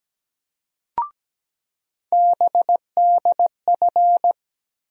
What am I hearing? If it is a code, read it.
Morse code, heard as a keyed tone, about 17 wpm, 700 Hz. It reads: BDF
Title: USA Call Sign Postfixes - 3 Letters 17wpm